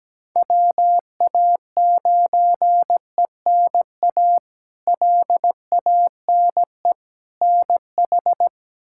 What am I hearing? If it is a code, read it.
Morse code, heard as a keyed tone, about 17 wpm, 700 Hz. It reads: WA9ENA LANE NH
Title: K1USN SST QSO Practice - 17wpm